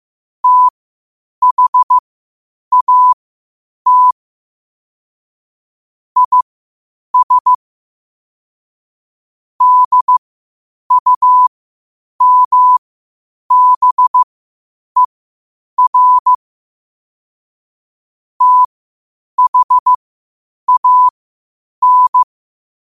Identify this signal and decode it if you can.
Morse code, heard as a keyed tone, about 15 wpm, spaced as for 8 wpm, 1000 Hz. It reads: THAT IS DUMBER THAN
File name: qotdfiles / 8wpm